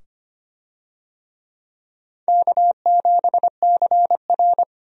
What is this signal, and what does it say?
Morse code, heard as a keyed tone, about 25 wpm, 700 Hz. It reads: K7CR
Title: USA Call Signs - 4 Characters - 25wpm